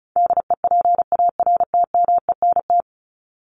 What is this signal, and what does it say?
Morse code, heard as a keyed tone, about 34 wpm, 700 Hz. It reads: DEPARTMENT
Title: Top 751-1000 Words 35wpm